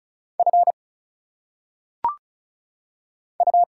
Morse code, heard as a keyed tone, about 36 wpm, 700 Hz. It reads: F U
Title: Single Letters - Phonetic Pronunciation 35wpm